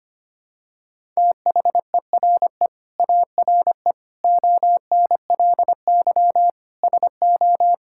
Morse code, heard as a keyed tone, about 25 wpm, 700 Hz. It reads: THERE ARE ONLY SO
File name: Sets of 9 Words - 25wpm